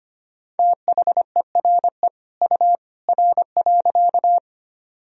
Morse code, heard as a keyed tone, about 25 wpm, 700 Hz. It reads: THERE U R.
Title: Sentences from Top 100 Words and Abbreviations 25wpm